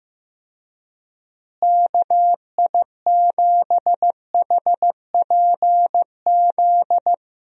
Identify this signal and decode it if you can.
Morse code, heard as a keyed tone, about 15 wpm, 700 Hz. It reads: KI7HPZ
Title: USA Call Signs - 6 Characters - Repeated 2x - 15wpm